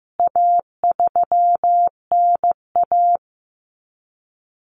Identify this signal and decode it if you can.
Morse code, heard as a keyed tone, about 15 wpm, 700 Hz. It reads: A3NA
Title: CWT Exchange 15wpm